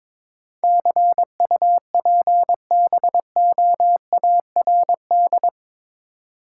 Morse code, heard as a keyed tone, about 22 wpm, 700 Hz. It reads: CUPBOARD
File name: practice / 22wpm / Eight Letter Words - 22wpm